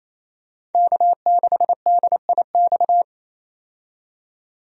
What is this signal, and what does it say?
Morse code, heard as a keyed tone, about 28 wpm, 700 Hz. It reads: K6DIX
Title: USA Call Signs - 5 Characters - 28wpm